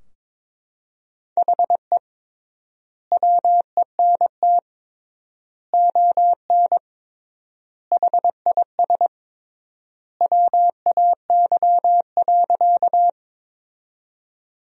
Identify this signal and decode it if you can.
Morse code, heard as a keyed tone, about 22 wpm, 700 Hz. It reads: HE WENT ON HIS WAY.
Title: 3x Word-Spacing - Sentences from Top 200 Words - 22wpm